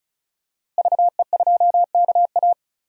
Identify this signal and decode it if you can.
Morse code, heard as a keyed tone, about 34 wpm, 700 Hz. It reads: VE2KA
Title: DXCC-100 Call Signs - 5 Characters - Repeated 2x - 35wpm